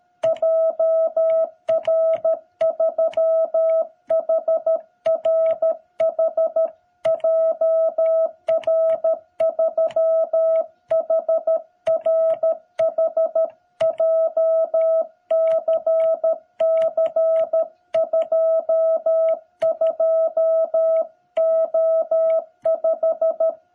Morse code, heard as a keyed tone, about 13 wpm, 660 Hz. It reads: JR3HRH JR3HRH JCC22O5
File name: jh8cbh / 2021cw